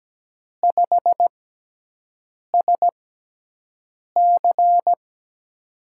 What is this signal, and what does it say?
Morse code, heard as a keyed tone, about 17 wpm, spaced as for 6 wpm, 700 Hz. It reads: 5SC
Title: USA Call Signs - 4 Characters - 6x Character Spacing - 17wpm